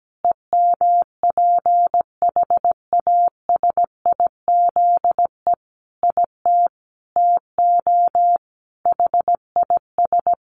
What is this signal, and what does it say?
Morse code, heard as a keyed tone, about 17 wpm, 700 Hz. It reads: EMPHASIZE IT TO HIS
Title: Sentences from Top 1500 Words 17wpm